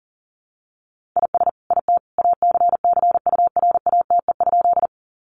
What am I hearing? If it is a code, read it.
Morse code, heard as a keyed tone, about 40 wpm, 700 Hz. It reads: IS IT ACCURATE?
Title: Sentences from Top 1200 Words 40wpm